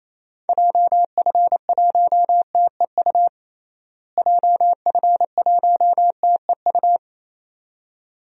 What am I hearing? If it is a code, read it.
Morse code, heard as a keyed tone, about 28 wpm, 700 Hz. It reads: JF1TEU JF1TEU
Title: DXCC-100 Call Signs - 6 Characters - Repeated 2x - 28wpm